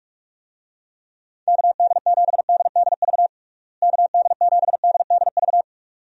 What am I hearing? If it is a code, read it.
Morse code, heard as a keyed tone, about 41 wpm, 700 Hz. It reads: KD7DDV KD7DDV
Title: USA Call Signs - 6 Characters - Repeated 2x - 45wpm